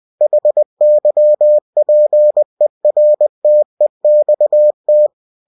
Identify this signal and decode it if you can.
Morse code, heard as a keyed tone, about 20 wpm, 600 Hz. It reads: HYPERTEXT
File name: cw-books / winnie-the-pooh / winnie-20wpm-page-0116